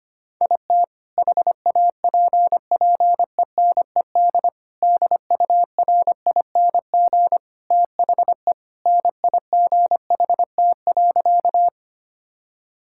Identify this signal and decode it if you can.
Morse code, heard as a keyed tone, about 25 wpm, 700 Hz. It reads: IT HAPPENED DURING THE NIGHT.
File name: practice / 25wpm / Sentences with 2-Word Phrases - 25wpm